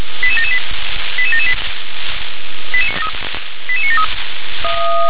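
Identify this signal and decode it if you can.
Five-tone selective call (ZVEI), heard as quick runs of five tones, tones of about 80 ms. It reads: 88778 88778 88973
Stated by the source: ZVEI five-tone selective call received by SDR.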